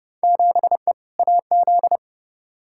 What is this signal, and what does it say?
Morse code, heard as a keyed tone, about 31 wpm, 700 Hz. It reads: 7E AZ
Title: Field Day Exchanges Encore - 30wpm